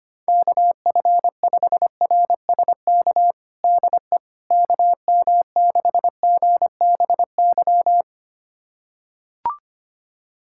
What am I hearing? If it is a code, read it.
Morse code, heard as a keyed tone, about 25 wpm, 700 Hz. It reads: KF5RSK DE KM6GBY
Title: USA Callsign DE Callsign - 6 Characters - 25wpm